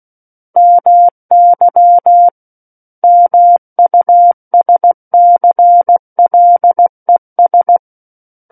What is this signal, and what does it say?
Morse code, heard as a keyed tone, about 16 wpm, 700 Hz. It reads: MY MUSCLES
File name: Book One / 07-Chapter 7 - 16 WPM